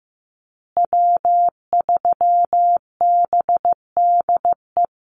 Morse code, heard as a keyed tone, about 15 wpm, 700 Hz. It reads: W3BDE
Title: USA Call Signs Repeated 2x 15wpm